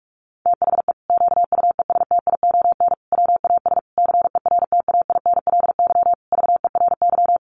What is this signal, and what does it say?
Morse code, heard as a keyed tone, about 44 wpm, 700 Hz. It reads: THE QUESTION WAS CERTAINLY VERY
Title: Sentences from Top 1200 Words 45wpm